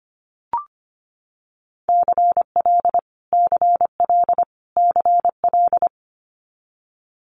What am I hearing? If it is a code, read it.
Morse code, heard as a keyed tone, about 25 wpm, 700 Hz. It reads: CL CL CL
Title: Multi-letter Phonemes Repeated 3x 25wpm